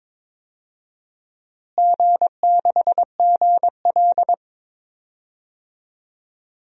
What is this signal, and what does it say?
Morse code, heard as a keyed tone, about 21 wpm, 700 Hz. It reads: G6GL
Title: DXCC-100 Call Signs - 4 Characters - Repeated 2x - 22wpm